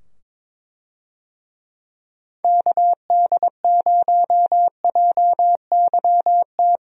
Morse code, heard as a keyed tone, about 23 wpm, 700 Hz. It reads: KD0JYT
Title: USA Call Signs - 6 Characters - 22wpm